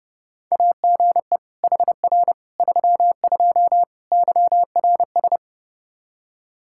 Morse code, heard as a keyed tone, about 32 wpm, 700 Hz. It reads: AGE HR 32 YRS
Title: Practice QSO - Exchange 3 - More Info - 30wpm